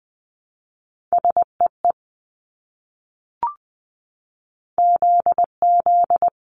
Morse code, heard as a keyed tone, about 19 wpm, 700 Hz. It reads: SEE ZZ
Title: Multi-letter Phonemes Repeated 3x 20wpm